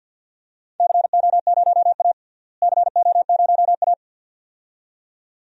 Morse code, heard as a keyed tone, about 52 wpm, 700 Hz. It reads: CO0A CO0A
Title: DXCC-100 Call Signs - 4 Characters - Repeated 2x - 50wpm